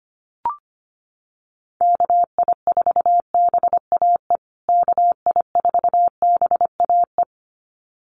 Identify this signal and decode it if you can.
Morse code, heard as a keyed tone, about 25 wpm, 700 Hz. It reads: KI4BAE KI4BAE
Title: USA Call Signs Repeated 2x 25wpm